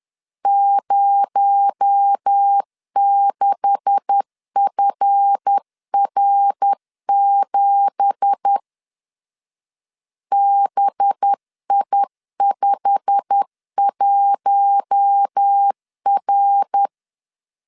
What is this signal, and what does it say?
Morse code, heard as a keyed tone, about 10 wpm, 800 Hz. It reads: 06FR7 BI51R
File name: LEZIONI / 9 2867 10wpm 2733